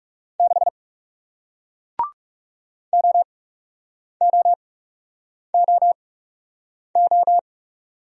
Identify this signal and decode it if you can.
Morse code, heard as a keyed tone, about 38 wpm, spaced as for 8 wpm, 700 Hz. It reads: 6 OOOO